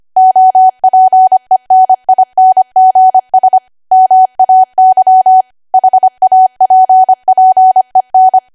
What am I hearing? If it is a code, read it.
Morse code, heard as a keyed tone, about 25 wpm, 750 Hz. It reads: OPENINGS MAY HAPPEN
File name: cwa / W1AW-Level3 / 25WPM